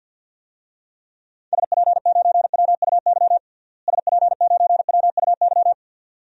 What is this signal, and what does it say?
Morse code, heard as a keyed tone, about 52 wpm, 700 Hz. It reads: SP9WUY SP9WUY
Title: DXCC-100 Call Signs - 6 Characters - Repeated 2x - 50wpm